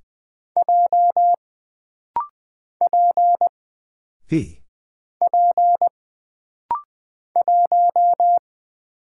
Morse code, heard as a keyed tone, about 20 wpm, 700 Hz. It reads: J P P 1
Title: Single Character (Letter:Number:Punctuation:Prosign) - Rapid-Fire - 20wpm